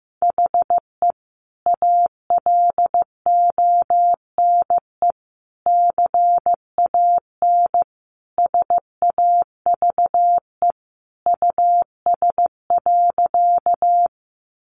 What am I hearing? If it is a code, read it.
Morse code, heard as a keyed tone, about 15 wpm, 700 Hz. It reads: HE ALONE CAN SAVE US.
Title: Sentences from Top 1100 Words 15wpm